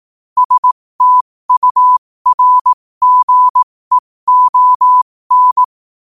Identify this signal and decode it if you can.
Morse code, heard as a keyed tone, about 18 wpm, spaced as for 15 wpm, 1000 Hz. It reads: STURGEON
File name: qotdfiles / 15wpm